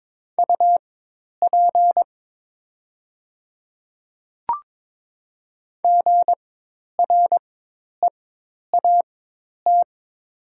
Morse code, heard as a keyed tone, about 22 wpm, spaced as for 10 wpm, 700 Hz. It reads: UP GREAT
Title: 4x Character Spacing - Top 200 Words - 22wpm